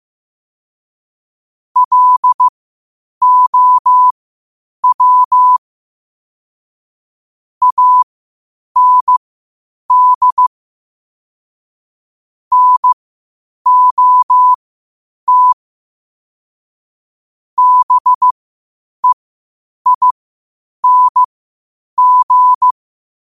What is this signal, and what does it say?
Morse code, heard as a keyed tone, about 15 wpm, spaced as for 8 wpm, 1000 Hz. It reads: LOW AND NOT BEING
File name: qotdfiles / 8wpm